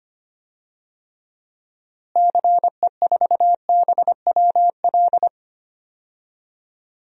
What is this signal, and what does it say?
Morse code, heard as a keyed tone, about 25 wpm, 700 Hz. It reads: CE4BWL
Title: DXCC-100 Call Signs - 6 Characters - Repeated 2x - 25wpm